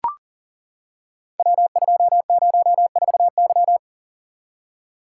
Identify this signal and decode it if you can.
Morse code, heard as a keyed tone, about 40 wpm, 690 Hz. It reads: W204Y
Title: Code Groups - 5 Letter:Numbers - 40wpm